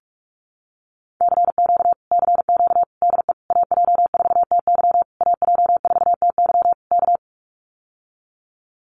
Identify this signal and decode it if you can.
Morse code, heard as a keyed tone, about 44 wpm, 700 Hz. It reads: CQ CQ DE AJ4TY AJ4TY K
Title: Common QSO Phrases Encore 45wpm